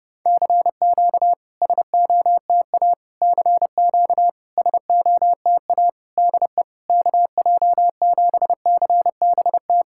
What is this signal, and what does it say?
Morse code, heard as a keyed tone, about 30 wpm, 700 Hz. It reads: CQ SOTA CQ SOTA DE KJ7CBT
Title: SOTA QSO Practice 30wpm